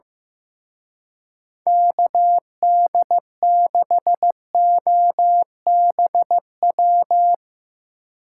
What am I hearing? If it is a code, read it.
Morse code, heard as a keyed tone, about 16 wpm, 700 Hz. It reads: KD6OBW
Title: Speed Racing - USA 2x3 Call Signs 15wpm